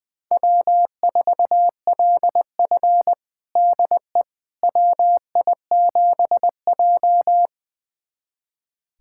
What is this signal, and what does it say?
Morse code, heard as a keyed tone, about 20 wpm, 700 Hz. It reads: W4LF DE WI7J
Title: USA Callsign DE Callsign - 4 Characters - 20wpm